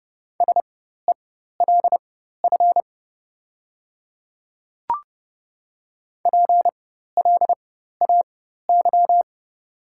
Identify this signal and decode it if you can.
Morse code, heard as a keyed tone, about 30 wpm, spaced as for 14 wpm, 700 Hz. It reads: SELF PLAY
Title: 4x Character Spacing - Top 200 Words - 30wpm